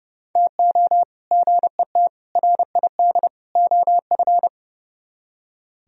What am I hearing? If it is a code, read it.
Morse code, heard as a keyed tone, about 28 wpm, 700 Hz. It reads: TO GET RID OF
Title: Speed-Racing - Sets of 4 Words 30wpm